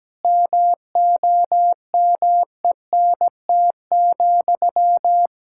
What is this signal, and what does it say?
Morse code, heard as a keyed tone, about 17 wpm, 700 Hz. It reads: MOMENT,
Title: Sentences from Top 1000 Words - Review Part 5 - 17wpm